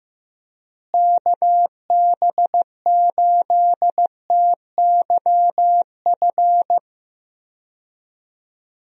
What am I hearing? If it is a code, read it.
Morse code, heard as a keyed tone, about 15 wpm, 700 Hz. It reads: KB8TYF
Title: USA Call Signs - 6 Characters - Repeated 2x - 15wpm